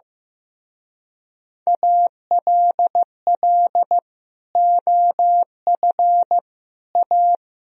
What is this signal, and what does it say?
Morse code, heard as a keyed tone, about 15 wpm, 700 Hz. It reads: ALL OF A
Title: Speed-Racing - Sets of 4 Words 15wpm